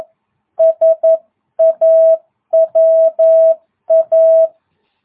Morse code, heard as a keyed tone, about 11 wpm, 650 Hz. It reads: SAWA